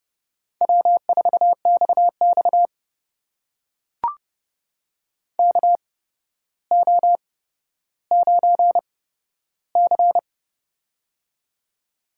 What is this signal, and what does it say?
Morse code, heard as a keyed tone, about 32 wpm, spaced as for 13 wpm, 700 Hz. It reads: W4XX KO9C
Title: USA Call Signs - 4 Characters - 8x Character Spacing - 30wpm